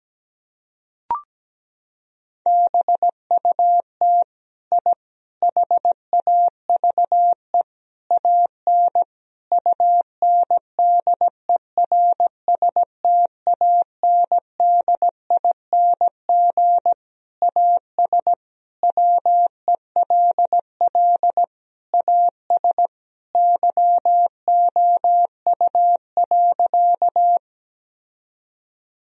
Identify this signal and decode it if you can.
Morse code, heard as a keyed tone, about 17 wpm, 700 Hz. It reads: BUT I HAVE AN UNDERSTANDING AS WELL AS YOU.
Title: Sentences from Top 1000 Words - Review Part 3 - 17wpm